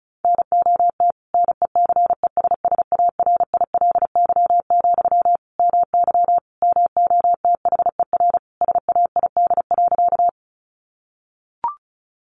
Morse code, heard as a keyed tone, about 34 wpm, 700 Hz. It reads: NOT NECESSARILY, MY MOTHER SAID.
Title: Sentences from Top 1400 Words 35wpm